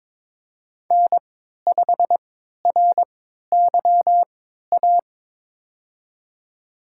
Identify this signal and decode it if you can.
Morse code, heard as a keyed tone, about 21 wpm, spaced as for 12 wpm, 700 Hz. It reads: N5RYA
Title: USA Call Signs - 3x Character Spacing - 22wpm